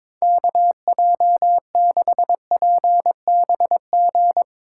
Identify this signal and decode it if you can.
Morse code, heard as a keyed tone, about 22 wpm, 700 Hz. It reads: KJ6PBG
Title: USA Call Signs - 6 Characters - 2x Character Spacing - 22wpm